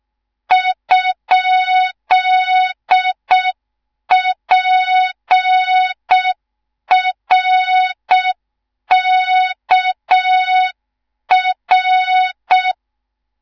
Morse code, heard as a keyed tone, about 6 wpm, 3700 Hz, with noise. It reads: ?PRKR
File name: Morze-HA2MN / hang / 18 IRASJELEK2